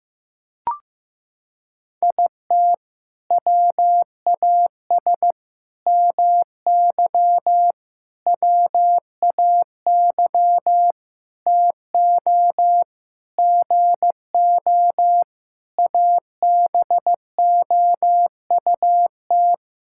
Morse code, heard as a keyed tone, about 15 wpm, 700 Hz. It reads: IT WAS MY WAY TO GO ABOUT
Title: Sentences from Top 1600 Words 15wpm